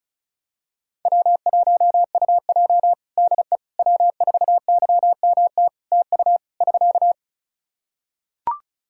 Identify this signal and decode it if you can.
Morse code, heard as a keyed tone, about 36 wpm, 700 Hz. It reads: W1UJ DE W4YMT TU <SK>
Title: Practice QSO - Exchange 4 - Ending - 35wpm